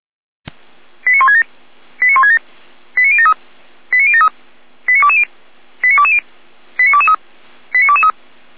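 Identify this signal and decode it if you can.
Five-tone selective call (ZVEI), heard as quick runs of five tones, tones of about 80 ms. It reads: 89167 89167 89073 89073 89220 89220 89223 89223
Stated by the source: ZVEI five-tone selective call 89022 received by SDR.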